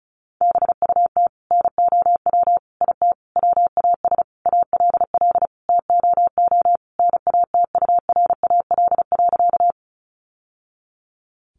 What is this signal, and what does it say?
Morse code, heard as a keyed tone, about 34 wpm, 700 Hz. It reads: BUT NOW IT WAS ALL TOO NATURAL.
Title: Sentences from Top 700 Words 35wpm